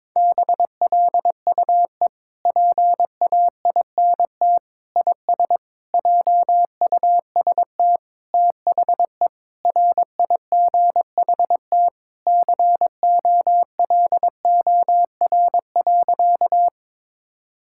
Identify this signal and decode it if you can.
Morse code, heard as a keyed tone, about 22 wpm, 700 Hz. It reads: BLUE PAINT IS JUST THE RIGHT COLOR.
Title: Sentences from Top 1000 Words - Review Part 6 - 22wpm